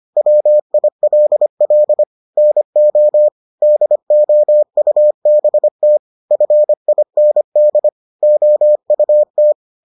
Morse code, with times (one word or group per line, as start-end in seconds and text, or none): 0.2-2.0 WILL
2.4-3.3 NO
3.6-6.0 DOUBT
6.3-7.9 FIND
8.2-9.5 OUT